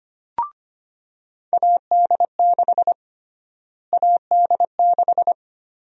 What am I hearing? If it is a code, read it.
Morse code, heard as a keyed tone, about 25 wpm, 700 Hz. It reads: AD6 AD6
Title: USA Call Sign Prefixes - Repeated 2x - 25wpm